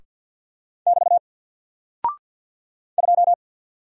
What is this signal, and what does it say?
Morse code, heard as a keyed tone, about 50 wpm, 700 Hz. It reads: <BT> 2
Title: Single Character (Letter:Number:Punctuation:Prosign) Alt 50wpm